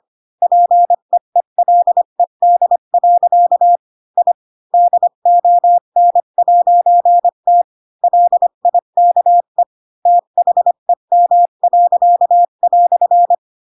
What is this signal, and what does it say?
Morse code, heard as a keyed tone, about 25 wpm, 700 Hz. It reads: PEELED. I DON'T LIKE THEM."
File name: Pinocchio / Pinocchio ch7 20 25 ik1hge